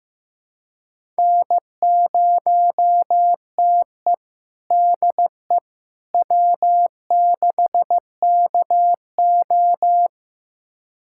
Morse code, heard as a keyed tone, about 15 wpm, 700 Hz. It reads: N0TE DE W6KO
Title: USA Callsign DE Callsign - 4 Characters - 15wpm